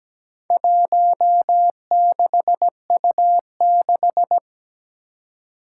Morse code, heard as a keyed tone, about 17 wpm, 700 Hz. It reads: 16U6
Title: Four Challenging Letter-Number Combinations 17wpm